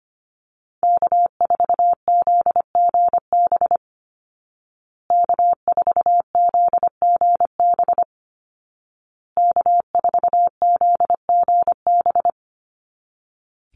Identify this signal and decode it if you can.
Morse code, heard as a keyed tone, about 25 wpm, 700 Hz. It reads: K4ZGB K4ZGB K4ZGB
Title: Call Signs Repeated 2x - 25wpm